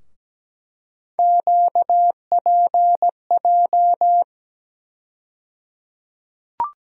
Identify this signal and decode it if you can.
Morse code, heard as a keyed tone, about 17 wpm, 700 Hz. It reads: QPJ
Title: USA Call Sign Postfixes - 3 Letters 17wpm